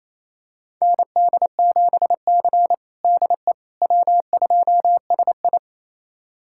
Morse code, heard as a keyed tone, about 28 wpm, 700 Hz. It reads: ND7C DE W2SI
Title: USA Callsign DE Callsign - 4 Characters - 28wpm